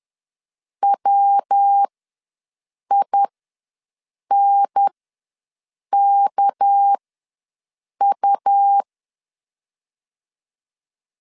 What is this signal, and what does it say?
Morse code, heard as a keyed tone, about 10 wpm, spaced as for 6 wpm, 800 Hz. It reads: WINKU
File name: LEZIONI / 6 PKX0 / 6 PK ws 10wpm 3745